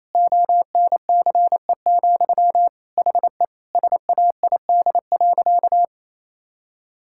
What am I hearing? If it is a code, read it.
Morse code, heard as a keyed tone, about 28 wpm, 700 Hz. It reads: ONCE, HE SAID.
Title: Sentences from Top 1600 Words 28wpm